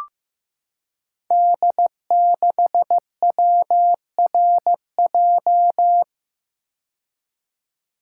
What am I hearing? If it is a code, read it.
Morse code, heard as a keyed tone, about 15 wpm, 700 Hz. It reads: D6WRJ
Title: Code Groups - 5 Letter:Numbers - 15wpm